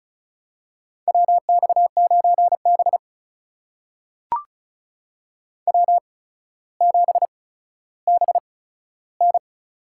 Morse code, heard as a keyed tone, about 36 wpm, spaced as for 15 wpm, 700 Hz. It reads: WX9B W7BN
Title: USA Call Signs - 4 Characters - 8x Character Spacing - 35wpm